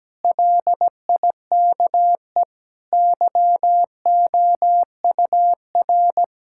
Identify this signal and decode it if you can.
Morse code, heard as a keyed tone, about 17 wpm, 700 Hz. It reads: LIKE YOUR
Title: Sentences from Top 1000 Words - Review Part 6 - 17wpm